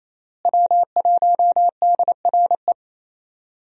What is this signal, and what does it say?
Morse code, heard as a keyed tone, about 28 wpm, 700 Hz. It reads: W1DRE
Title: USA Call Signs - 5 Characters - Repeated 2x - 28wpm